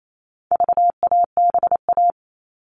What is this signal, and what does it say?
Morse code, heard as a keyed tone, about 28 wpm, 700 Hz. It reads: VABA
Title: All Cut Numbers - 4-Digit 28wpm